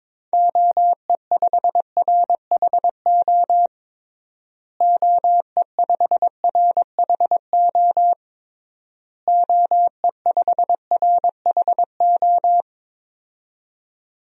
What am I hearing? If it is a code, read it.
Morse code, heard as a keyed tone, about 22 wpm, 700 Hz. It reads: OE5RHO OE5RHO OE5RHO
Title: DXCC-100 Call Signs - 6 Characters - Repeated 3x - 22wpm